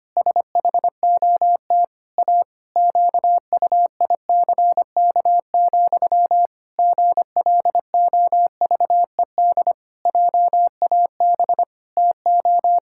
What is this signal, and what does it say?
Morse code, heard as a keyed tone, about 25 wpm, 700 Hz. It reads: SHOT A QUICK, GLOVED JAB TO
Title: Pangram Challenge v2 - 25wpm